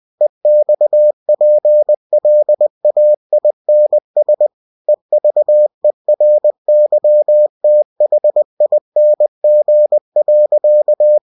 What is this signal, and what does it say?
Morse code, heard as a keyed tone, about 20 wpm, 600 Hz. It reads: EXPLAINS EVERYTHING.
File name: cw-books / winnie-the-pooh / winnie-20wpm-page-0027